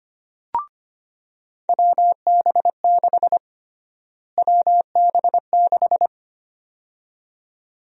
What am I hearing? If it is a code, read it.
Morse code, heard as a keyed tone, about 25 wpm, 700 Hz. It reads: WB6 WB6
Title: USA Call Sign Prefixes - Repeated 2x - 25wpm